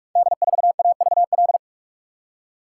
Morse code, heard as a keyed tone, about 44 wpm, 700 Hz. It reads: D4AVL
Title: Code Groups - 5 Letter:Numbers - 45wpm